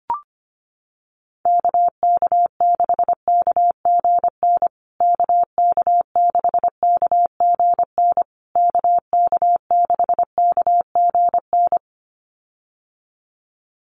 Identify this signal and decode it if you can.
Morse code, heard as a keyed tone, about 25 wpm, 700 Hz. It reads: KK6KGN KK6KGN KK6KGN
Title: USA Call Signs Repeated 3x 25wpm